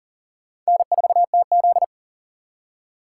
Morse code, heard as a keyed tone, about 40 wpm, 700 Hz. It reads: N4TZ
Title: K1USN SST QSO Practice - 40wpm